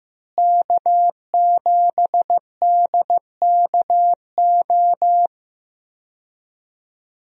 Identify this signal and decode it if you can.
Morse code, heard as a keyed tone, about 15 wpm, 700 Hz. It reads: K7DKO
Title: USA Call Signs - 5 Characters - Repeated 2x - 15wpm